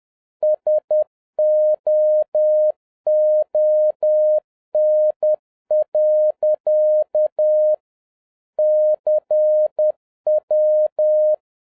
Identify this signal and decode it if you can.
Morse code, heard as a keyed tone, about 10 wpm, 610 Hz. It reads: SOON. CW
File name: 2011-07-mp3 / n21 GREETINGS FROM SINGAPORE 10wpm